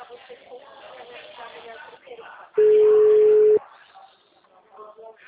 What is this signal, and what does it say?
Ringback tone (Europe, 425 Hz), with speech.